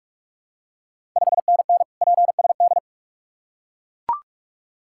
Morse code, heard as a keyed tone, about 44 wpm, 690 Hz. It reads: 5NN PSD
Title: Ontario QSO Party Exchange 45wpm